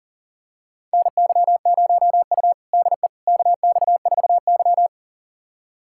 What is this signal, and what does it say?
Morse code, heard as a keyed tone, about 37 wpm, 700 Hz. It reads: NY0U DE KX4Y
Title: USA Callsign DE Callsign - 4 Characters - 40wpm